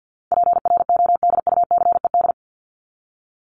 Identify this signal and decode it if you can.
Morse code, heard as a keyed tone, about 50 wpm, 700 Hz. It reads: PRODUCED
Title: ED Suffix - Top 100 Words with Base Words 50wpm